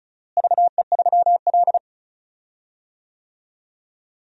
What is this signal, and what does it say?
Morse code, heard as a keyed tone, about 34 wpm, 700 Hz. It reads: VE3L
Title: DXCC-100 Call Signs - 4 Characters - Repeated 3x - 35wpm